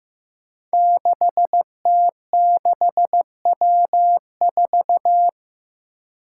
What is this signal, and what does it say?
Morse code, heard as a keyed tone, about 15 wpm, 700 Hz. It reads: 6T6W4
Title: Five Challenging Letter-Number Combinations 15wpm